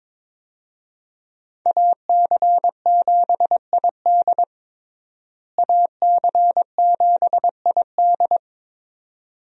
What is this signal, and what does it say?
Morse code, heard as a keyed tone, about 22 wpm, 700 Hz. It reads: AC7ID AC7ID
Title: USA Call Signs - 5 Characters - Repeated 2x - 22wpm